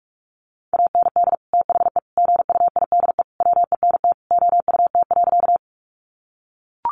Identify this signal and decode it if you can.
Morse code, heard as a keyed tone, about 44 wpm, 700 Hz. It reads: AND THE GUIDE WENT OUT.
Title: Sentences from Top 600 Words 45wpm